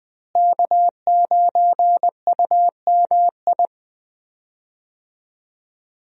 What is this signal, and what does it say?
Morse code, heard as a keyed tone, about 20 wpm, 700 Hz. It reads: K9UMI
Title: USA Call Signs - 5 Characters - Repeated 3x - 20wpm